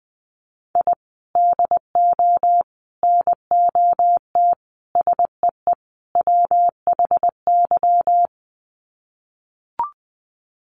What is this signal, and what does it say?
Morse code, heard as a keyed tone, about 20 wpm, 700 Hz. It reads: I DO NOT SEE WHY
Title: Sets of 5 Words Encore 20wpm